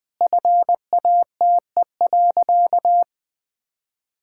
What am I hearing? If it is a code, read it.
Morse code, heard as a keyed tone, about 20 wpm, 690 Hz. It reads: FATE.
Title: Sentences with 4-Word Phrases - Spoken Beginning - 20wpm